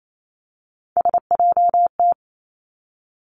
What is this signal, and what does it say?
Morse code, heard as a keyed tone, about 28 wpm, 700 Hz. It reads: SJT
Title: Three Challenging Letter-Number Combinations 28wpm